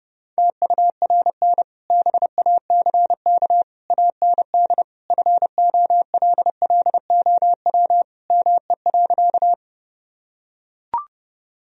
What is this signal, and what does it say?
Morse code, heard as a keyed tone, about 30 wpm, 700 Hz. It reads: TURN BACK AND FOLLOW ME.
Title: Sentences from Top 1000 Words - Review Part 1 - 30wpm